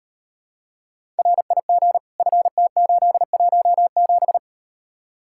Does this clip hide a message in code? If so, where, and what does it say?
Yes: Morse code from 1.2 to 4.4 s, heard as a keyed tone, about 38 wpm, 700 Hz. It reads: RIG FT817